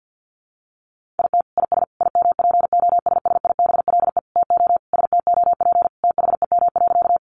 Morse code, heard as a keyed tone, about 50 wpm, 700 Hz. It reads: IT IS IMPOSSIBLE TO STOP THEM.
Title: Sentences from Top 1100 Words 50wpm